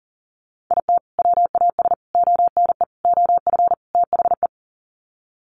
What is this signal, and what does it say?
Morse code, heard as a keyed tone, about 40 wpm, 700 Hz. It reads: IT WAS ONE OF THE
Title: Sets of 5 Words 40wpm